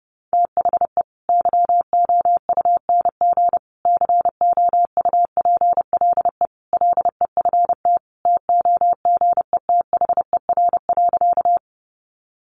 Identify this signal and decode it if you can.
Morse code, heard as a keyed tone, about 28 wpm, 700 Hz. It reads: THE YOUNG COUPLE LEFT TOGETHER.
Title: Sentences from Top 1100 Words 30wpm